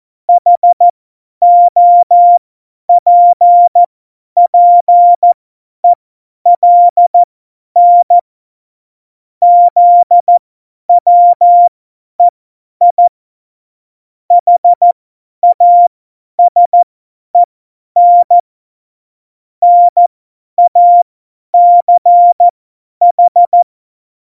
Morse code, heard as a keyed tone, about 14 wpm, spaced as for 10 wpm, 700 Hz. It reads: HOPPELN ZWEI HASEN NACH
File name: Morsekurs / Witze / Witz 07 April 2020, 50 F70